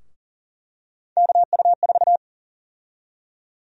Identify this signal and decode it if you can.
Morse code, heard as a keyed tone, about 38 wpm, 700 Hz. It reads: KU4
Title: USA Call Sign Prefixes - 40wpm